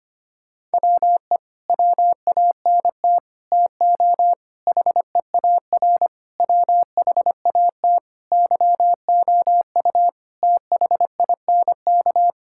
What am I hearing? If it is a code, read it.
Morse code, heard as a keyed tone, about 25 wpm, 700 Hz. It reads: WE WANT TO HEAR WHAT YOU THINK
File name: practice / 25wpm / Sentences with 9-Word Phrases - 25wpm